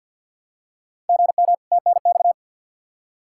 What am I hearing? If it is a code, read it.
Morse code, heard as a keyed tone, about 48 wpm, 700 Hz. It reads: GM TNX
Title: Practice QSO - Exchange 1 - Beginning - 50wpm